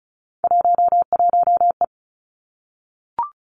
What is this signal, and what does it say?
Morse code, heard as a keyed tone, about 36 wpm, 700 Hz. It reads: 11E
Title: Three Challenging Letter-Number Combinations 35wpm